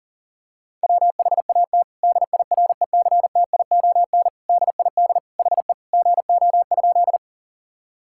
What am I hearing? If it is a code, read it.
Morse code, heard as a keyed tone, about 40 wpm, 700 Hz. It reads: WHAT DIRECTION DID HE GO?
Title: Sentences from Top 1000 Words - Review Part 5 - 40wpm